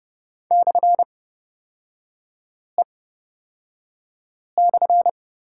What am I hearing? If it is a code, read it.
Morse code, heard as a keyed tone, about 30 wpm, spaced as for 5 wpm, 700 Hz. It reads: /E/
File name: Lesson 36e - BT - Three Characters 30wpm